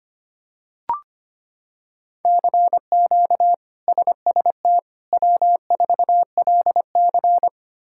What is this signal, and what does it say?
Morse code, heard as a keyed tone, about 25 wpm, 700 Hz. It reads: CQ SST W4LC
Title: K1USN SST QSO Practice - 25wpm